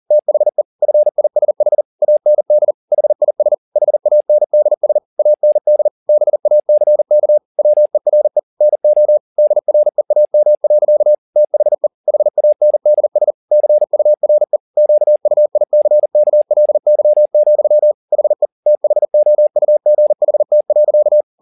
Morse code, heard as a keyed tone, about 40 wpm, 600 Hz. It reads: THE FISH AND HIS HANDS AND BACK WERE NO DREAM. THE HANDS CURE QUICKLY, HE THOUGHT.
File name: the old man and the sea 2 40wpm